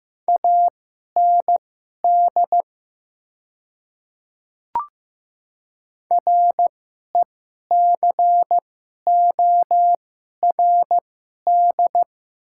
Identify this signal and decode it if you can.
Morse code, heard as a keyed tone, about 15 wpm, spaced as for 11 wpm, 700 Hz. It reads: AND RECORD
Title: 2x Character Spacing - Top 500 Words - 15wpm